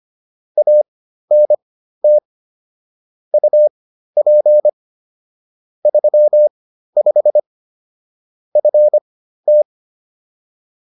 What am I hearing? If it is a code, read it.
Morse code, heard as a keyed tone, about 25 wpm, spaced as for 13 wpm, 600 Hz. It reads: ANT UP 35 FT